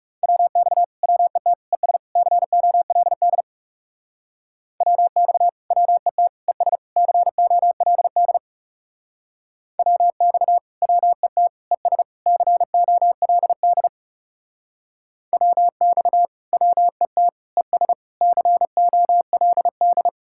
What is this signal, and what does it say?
Morse code, heard as a keyed tone, about 36 wpm, 700 Hz. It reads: WX WET ES COLD WX WET ES COLD WX WET ES COLD WX WET ES COLD
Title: Speed-Racing Common QSO Phrases - 30wpm